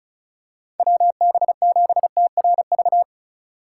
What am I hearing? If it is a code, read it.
Morse code, heard as a keyed tone, about 36 wpm, 700 Hz. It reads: WB7TRV
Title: USA Call Signs - 4x Character Spacing - 35wpm